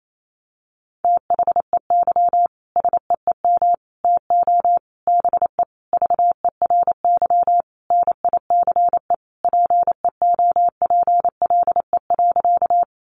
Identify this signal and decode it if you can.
Morse code, heard as a keyed tone, about 28 wpm, 700 Hz. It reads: THEY SEEM TO BE VERY NICE PEOPLE.
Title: Sentences from Top 1100 Words 28wpm